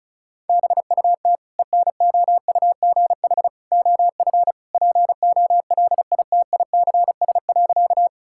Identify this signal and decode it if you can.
Morse code, heard as a keyed tone, about 34 wpm, 700 Hz. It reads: BUT ENOUGH OF POLITICS.
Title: Sentences from Top 1000 Words - Review Part 6 - 35wpm